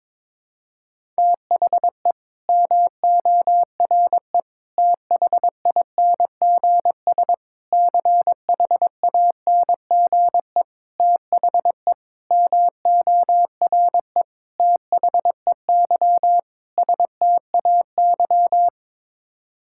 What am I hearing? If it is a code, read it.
Morse code, heard as a keyed tone, about 22 wpm, 700 Hz. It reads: THE MORE THINGS CHANGE THE MORE THEY STAY
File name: practice / 22wpm / Sets of 8 Words - 22wpm